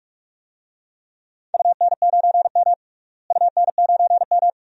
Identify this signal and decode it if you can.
Morse code, heard as a keyed tone, about 46 wpm, 700 Hz. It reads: UN9M UN9M
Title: DXCC-100 Call Signs - 4 Characters - Repeated 2x - 45wpm